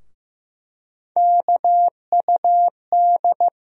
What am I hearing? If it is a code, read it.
Morse code, heard as a keyed tone, about 15 wpm, 700 Hz. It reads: KUD
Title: USA Call Sign Postfixes - 3 Letters 15wpm